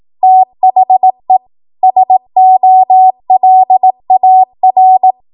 Morse code, heard as a keyed tone, about 18 wpm, 750 Hz. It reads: THE SOLAR
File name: Archive / 18 WPM / 18WPM